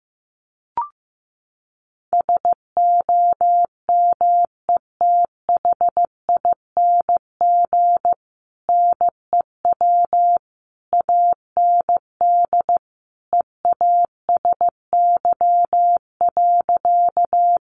Morse code, heard as a keyed tone, about 15 wpm, 700 Hz. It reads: SOMETHING NEW AND EASY.
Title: Sentences from Top 700 Words 15wpm